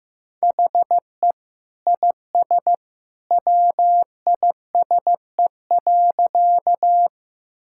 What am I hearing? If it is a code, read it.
Morse code, heard as a keyed tone, about 15 wpm, 700 Hz. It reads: HE IS WISE.
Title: Sentences from Top 1600 Words 15wpm